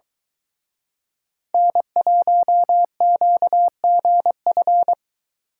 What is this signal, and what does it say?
Morse code, heard as a keyed tone, about 23 wpm, 700 Hz. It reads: N1QGF